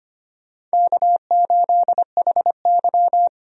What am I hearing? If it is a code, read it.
Morse code, heard as a keyed tone, about 25 wpm, 700 Hz. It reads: K8HY
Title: USA Call Signs - 4 Characters - Repeated 2x - 25wpm